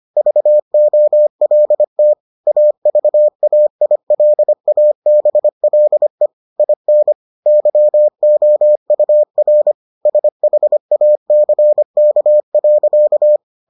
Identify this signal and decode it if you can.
Morse code, heard as a keyed tone, about 25 wpm, 600 Hz. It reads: VOLT AVAILABLE IN YOUR SHACK.